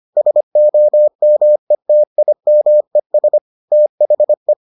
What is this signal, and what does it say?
Morse code, heard as a keyed tone, about 25 wpm, 600 Hz. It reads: SOMETIMES THE